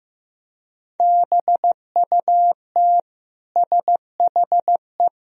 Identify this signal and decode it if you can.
Morse code, heard as a keyed tone, about 15 wpm, 700 Hz. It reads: BUT SHE
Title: Sentences from Top 1000 Words - Review Part 1 - 15wpm